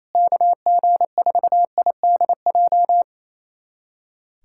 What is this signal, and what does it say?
Morse code, heard as a keyed tone, about 28 wpm, 700 Hz. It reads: KG4IDJ
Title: USA Callsign DE Callsign 28wpm